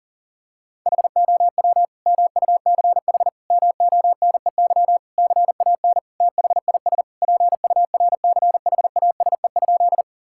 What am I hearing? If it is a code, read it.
Morse code, heard as a keyed tone, about 40 wpm, 700 Hz. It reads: HOW MUCH MONEY CAN THIS PURCHASE?
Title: Sentences from Top 1000 Words - Review Part 5 - 40wpm